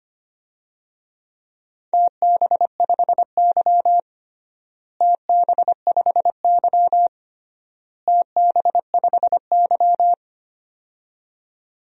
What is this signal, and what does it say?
Morse code, heard as a keyed tone, about 25 wpm, 700 Hz. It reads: TB5Y TB5Y TB5Y
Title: DXCC-100 Call Signs - 4 Characters - Repeated 3x - 25wpm